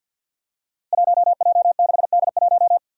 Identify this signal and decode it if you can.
Morse code, heard as a keyed tone, about 46 wpm, 700 Hz. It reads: 1J6D1
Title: Five Challenging Letter-Number Combinations 50wpm